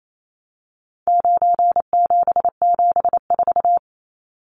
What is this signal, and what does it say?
Morse code, heard as a keyed tone, about 28 wpm, 700 Hz. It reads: 9774
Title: Numbers - 4-Digit 28wpm